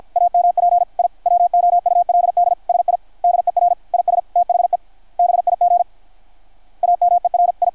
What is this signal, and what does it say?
Morse code, heard as a keyed tone, about 53 wpm, 690 Hz. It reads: AMP I WOULD SIDER IS THE BIG AMERI
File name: archived / CRSnet